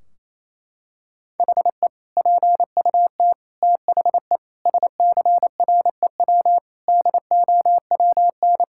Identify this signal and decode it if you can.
Morse code, heard as a keyed tone, about 28 wpm, 700 Hz. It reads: HE PUT THE SCREW DOWN
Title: Sentences from Top 1600 Words 28wpm